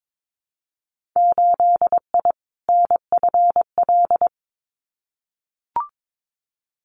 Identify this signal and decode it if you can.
Morse code, heard as a keyed tone, about 22 wpm, 700 Hz. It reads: 8I NFL
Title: Winter Field Day Exchanges Encore 22wpm